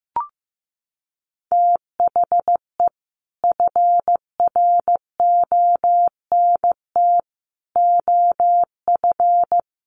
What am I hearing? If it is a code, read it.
Morse code, heard as a keyed tone, about 15 wpm, 700 Hz. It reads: THE FRONT OF
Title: Sets of 4 Words 15wpm